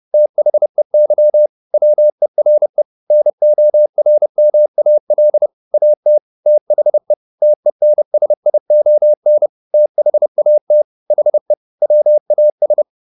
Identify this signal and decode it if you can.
Morse code, heard as a keyed tone, about 30 wpm, 600 Hz. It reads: THEY WERE NORMAL AT THE TENSION THAT HE WAS